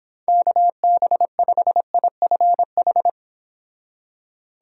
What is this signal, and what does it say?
Morse code, heard as a keyed tone, about 26 wpm, 700 Hz. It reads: KB5IFH